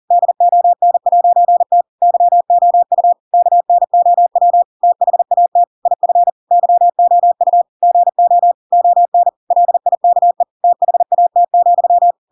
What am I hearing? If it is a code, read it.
Morse code, heard as a keyed tone, about 40 wpm, 700 Hz. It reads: DON'T YOU KNOW THAT IF YOU GO ON LIKE THAT,
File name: Pinocchio / Pinocchio ch4 40 40 ik1hge